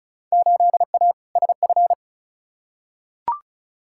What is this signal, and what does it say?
Morse code, heard as a keyed tone, about 36 wpm, 700 Hz. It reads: 8A SF
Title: Field Day Exchanges Encore - 35wpm